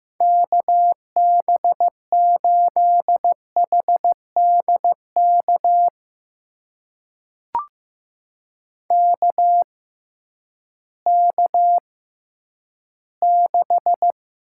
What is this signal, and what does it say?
Morse code, heard as a keyed tone, about 15 wpm, spaced as for 10 wpm, 700 Hz. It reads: KB8HDK KK6
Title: USA Call Signs - 6x Character Spacing - 15wpm